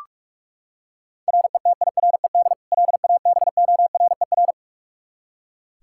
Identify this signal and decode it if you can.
Morse code, heard as a keyed tone, about 42 wpm, 700 Hz. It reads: RETIRED LABORER
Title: Practice QSO - Exchange 3 - More Info - 45wpm